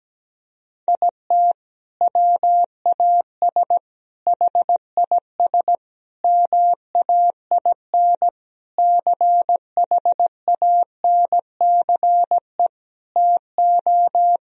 Morse code, heard as a keyed tone, about 17 wpm, 700 Hz. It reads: IT WAS HIS MAIN CHANCE TO
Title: Sentences from Top 1000 Words - Review Part 2 - 17wpm